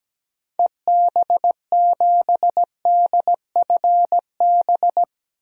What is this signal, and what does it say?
Morse code, heard as a keyed tone, about 17 wpm, 700 Hz. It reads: EB7DFB
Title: DXCC-100 Call Signs - 6 Characters - 17wpm